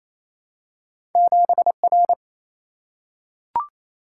Morse code, heard as a keyed tone, about 28 wpm, 700 Hz. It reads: 7R
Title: Single-Letter Single-Number 28wpm